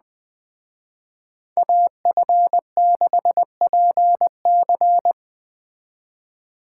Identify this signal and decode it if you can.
Morse code, heard as a keyed tone, about 20 wpm, 700 Hz. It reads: AF6PC
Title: Speed Racing - USA 2x2 Call Signs 17wpm